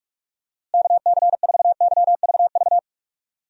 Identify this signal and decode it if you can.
Morse code, heard as a keyed tone, about 41 wpm, 700 Hz. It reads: KC4YVV
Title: USA Call Signs - 6 Characters - Repeated 3x - 45wpm